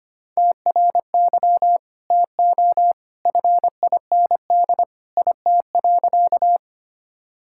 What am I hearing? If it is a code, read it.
Morse code, heard as a keyed tone, about 25 wpm, 700 Hz. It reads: TRY TO FIND IT.
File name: practice / 25wpm / Sentences with 2-Word Phrases - 25wpm